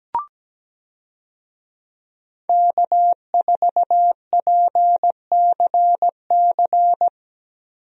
Morse code, heard as a keyed tone, about 17 wpm, 700 Hz. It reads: K4PCC
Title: USA Call Signs - 5 Characters - Repeated 3x - 17wpm